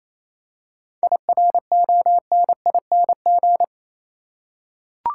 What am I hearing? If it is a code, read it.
Morse code, heard as a keyed tone, about 27 wpm, 700 Hz. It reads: IRONING
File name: Easy Words - Speed-Racing with Hints - 22wpm